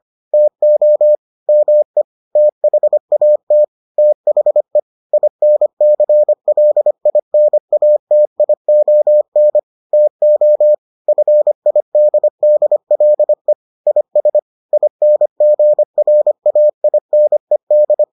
Morse code, heard as a keyed tone, about 25 wpm, 600 Hz. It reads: TO ME THAT THE INCLINATION TO FIDDLE IS INGRAINED